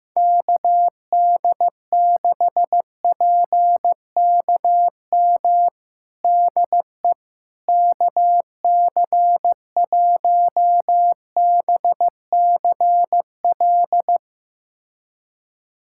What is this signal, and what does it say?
Morse code, heard as a keyed tone, about 16 wpm, 700 Hz. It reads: KD6PKM DE KC1BCL
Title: USA Callsign DE Callsign 15wpm